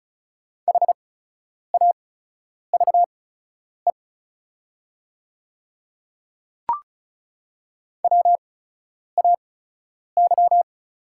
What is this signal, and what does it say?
Morse code, heard as a keyed tone, about 36 wpm, spaced as for 10 wpm, 700 Hz. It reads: HAVE WAY
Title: 8x Character Spacing - Top 100 Words - 35wpm